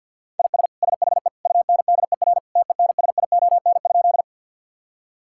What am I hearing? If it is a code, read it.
Morse code, heard as a keyed tone, about 50 wpm, 700 Hz. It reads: IS SHE UNDER TENSION?
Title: Sentences from Top 1600 Words 50wpm